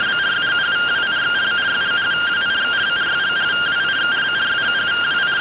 Two-tone teleprinter signal, about 170 Hz shift.